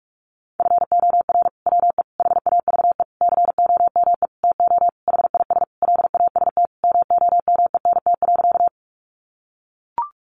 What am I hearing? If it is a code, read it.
Morse code, heard as a keyed tone, about 44 wpm, 700 Hz. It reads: FOR WE HAVE COME TO HIS LAST MOMENT.
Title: Sentences from Top 1000 Words 45wpm